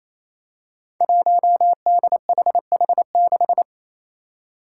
Morse code, heard as a keyed tone, about 28 wpm, 700 Hz. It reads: 1DHH6
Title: Five Challenging Letter-Number Combinations 28wpm